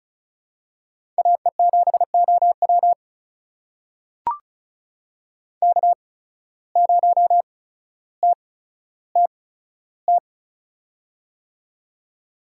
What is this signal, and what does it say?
Morse code, heard as a keyed tone, about 36 wpm, spaced as for 15 wpm, 700 Hz. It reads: AE7OW K0TTT
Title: USA Call Signs - 5 Characters - 8x Character Spacing - 35wpm